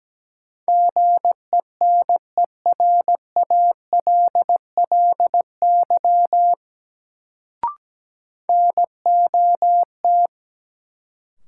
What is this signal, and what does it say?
Morse code, heard as a keyed tone, about 17 wpm, 700 Hz. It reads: GENERALLY NOT
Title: Sentences from Top 600 Words - New Word Drill - 17wpm